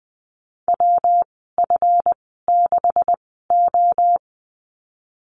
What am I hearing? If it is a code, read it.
Morse code, heard as a keyed tone, about 20 wpm, spaced as for 15 wpm, 700 Hz. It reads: WF6O
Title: 147 Lesson 24g - Letter y - Call Signs 20wpm